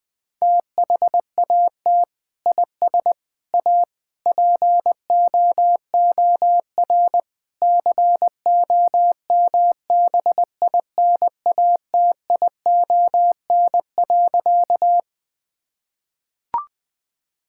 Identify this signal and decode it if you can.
Morse code, heard as a keyed tone, about 20 wpm, 700 Hz. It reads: THAT IS A POOR COMBINATION.
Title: Sentences from Top 1000 Words - Review Part 6 - 20wpm